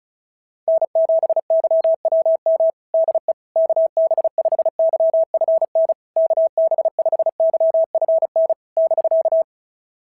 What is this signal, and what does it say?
Morse code, heard as a keyed tone, about 33 wpm, 650 Hz. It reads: N7YWM DE KB5YFN KB5YFN <BK>
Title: SOTA QSO Practice 35wpm